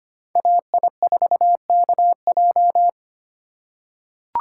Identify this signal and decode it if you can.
Morse code, heard as a keyed tone, about 25 wpm, 700 Hz. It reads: AI4KJ
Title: USA Call Signs - 5 Characters - 8x Character Spacing - 25wpm